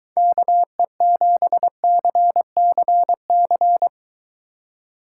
Morse code, heard as a keyed tone, about 23 wpm, 700 Hz. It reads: KE7CCC